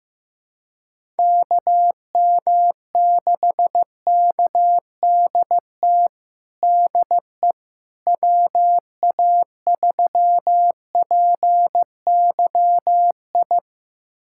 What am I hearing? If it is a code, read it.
Morse code, heard as a keyed tone, about 15 wpm, 700 Hz. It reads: KM6KDT DE WA3PYI
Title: USA Callsign DE Callsign - 6 Characters - 15wpm